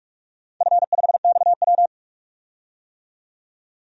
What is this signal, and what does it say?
Morse code, heard as a keyed tone, about 42 wpm, 700 Hz. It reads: F5XW
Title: DXCC-100 Call Signs - 4 Characters - 45wpm